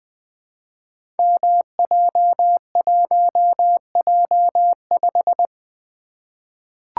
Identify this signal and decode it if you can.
Morse code, heard as a keyed tone, about 20 wpm, 700 Hz. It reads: MJ1J5
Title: Code Groups - 5 Letter:Numbers - 20wpm